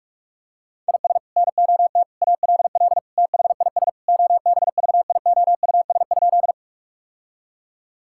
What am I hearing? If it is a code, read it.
Morse code, heard as a keyed tone, about 46 wpm, 700 Hz. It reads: IS NOT ALL THIS OBVIOUS?
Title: Sentences from Top 1600 Words 45wpm